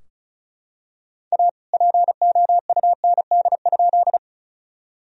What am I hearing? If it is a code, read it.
Morse code, heard as a keyed tone, about 36 wpm, 700 Hz. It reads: A POUND?
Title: Sentences from Top 1000 Words - Review Part 2 - 35wpm